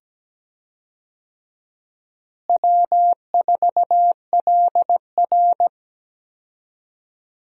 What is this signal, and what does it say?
Morse code, heard as a keyed tone, about 17 wpm, 700 Hz. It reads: W4LR
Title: USA Call Signs - 4 Characters - Repeated 2x - 17wpm